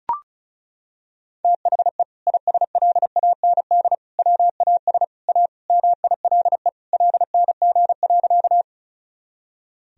Morse code, heard as a keyed tone, about 36 wpm, 700 Hz. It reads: THE ISLAND WAS A MILE LNG.
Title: Sentences from Top 500 Words and Abbreviations 35wpm